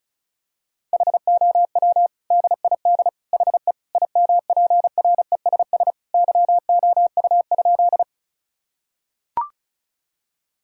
Morse code, heard as a keyed tone, about 36 wpm, 700 Hz. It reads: HOW DID HE IMPRESS YOU?
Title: Sentences from Top 1600 Words 35wpm